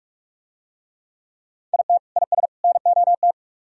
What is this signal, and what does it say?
Morse code, heard as a keyed tone, about 42 wpm, 700 Hz. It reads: IT IS NOT